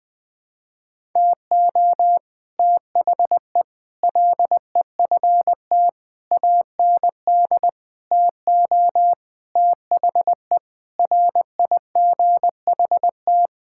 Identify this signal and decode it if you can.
Morse code, heard as a keyed tone, about 20 wpm, 700 Hz. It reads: TO THE LEFT AND TO THE RIGHT
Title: Sentences with 9-Word Phrases - 20wpm